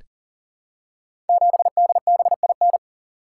Morse code, heard as a keyed tone, about 43 wpm, 700 Hz. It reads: 7DBIN
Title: Code Groups - 5 Letter:Numbers - 40wpm